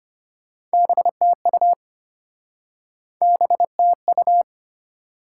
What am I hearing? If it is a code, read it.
Morse code, heard as a keyed tone, about 33 wpm, 700 Hz. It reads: BTU BTU